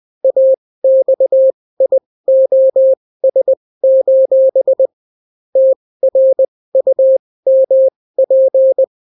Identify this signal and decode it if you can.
Morse code, heard as a keyed tone, about 20 wpm, spaced as for 16 wpm, 530 Hz. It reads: AXIOS: TRUMP